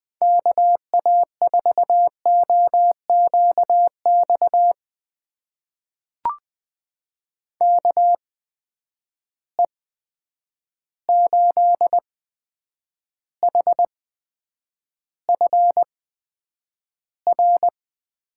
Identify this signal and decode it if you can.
Morse code, heard as a keyed tone, about 20 wpm, spaced as for 9 wpm, 700 Hz. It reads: KA4OQX KE8HFR
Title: USA Call Signs - 6 Characters - 8x Character Spacing - 20wpm